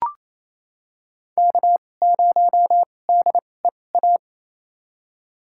Morse code, heard as a keyed tone, about 28 wpm, spaced as for 20 wpm, 700 Hz. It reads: K0DEA
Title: USA Call Signs - 5 Characters - 2x Character Spacing - 28wpm